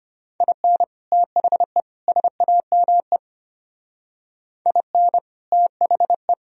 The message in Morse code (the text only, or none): IN THE SAME IN THE